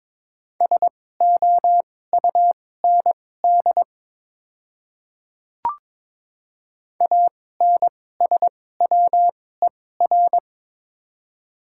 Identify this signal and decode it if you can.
Morse code, heard as a keyed tone, about 23 wpm, spaced as for 16 wpm, 700 Hz. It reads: SOUND ANSWER
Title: 2x Character Spacing - Top 500 Words - 22wpm